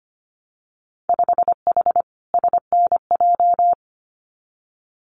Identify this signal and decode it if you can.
Morse code, heard as a keyed tone, about 25 wpm, 690 Hz. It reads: 5H SNJ
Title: Winter Field Day Exchanges Encore 25wpm